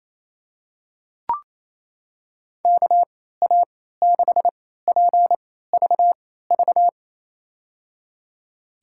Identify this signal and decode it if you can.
Morse code, heard as a keyed tone, about 28 wpm, spaced as for 16 wpm, 700 Hz. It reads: KA6PVV
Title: USA Call Signs - 3x Character Spacing - 28wpm